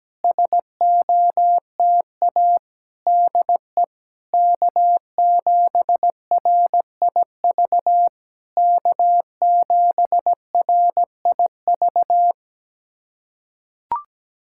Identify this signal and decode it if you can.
Morse code, heard as a keyed tone, about 17 wpm, 700 Hz. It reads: SOTA DE K7RIV K7RIV
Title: SOTA QSO Practice 17wpm